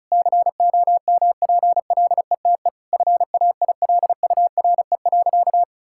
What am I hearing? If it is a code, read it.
Morse code, heard as a keyed tone, about 36 wpm, 700 Hz. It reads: COMPLETE FAILURE.
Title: Sentences from Top 1000 Words - Review Part 6 - 35wpm